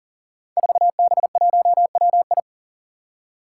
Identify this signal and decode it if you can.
Morse code, heard as a keyed tone, about 37 wpm, 700 Hz. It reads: 4B1WI
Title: Five Challenging Letter-Number Combinations 40wpm